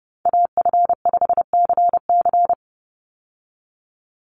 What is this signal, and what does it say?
Morse code, heard as a keyed tone, about 30 wpm, 700 Hz. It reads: AF5CC
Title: Call Signs Repeated 3x - 30wpm